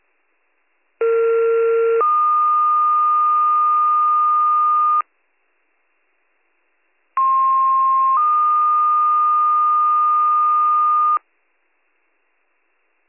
Two-tone page, heard as a one-second tone, then a three-second tone, about 460 then 1220 Hz, about 1060 then 1220 Hz. It is heard from 1.0 to 11.2 s.